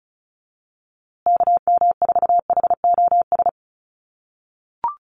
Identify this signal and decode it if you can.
Morse code, heard as a keyed tone, about 34 wpm, 700 Hz. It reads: KM4HOS
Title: USA Call Signs Repeated 3x 35wpm